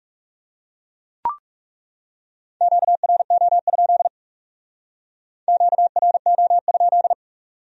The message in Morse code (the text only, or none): QRO? QRO?